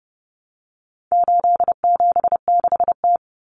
Morse code, heard as a keyed tone, about 30 wpm, 700 Hz. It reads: 876T
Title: Common Cut Numbers - 4-Digit 30wpm